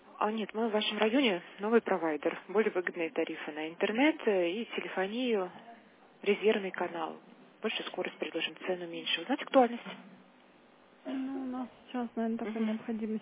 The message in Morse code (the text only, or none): none